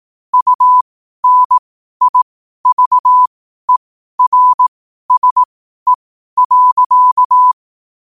Morse code, heard as a keyed tone, about 18 wpm, spaced as for 12 wpm, 1000 Hz. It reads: UNIVERSE.
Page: New quotes every day in morse code at 12 Words per minute.